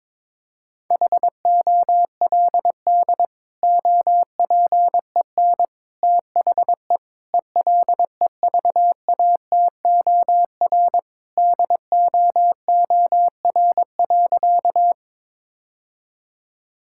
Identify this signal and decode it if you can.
Morse code, heard as a keyed tone, about 23 wpm, 700 Hz. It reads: HOLD OPEN THE ELEVATOR DOOR.
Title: Sentences from Top 1600 Words 22wpm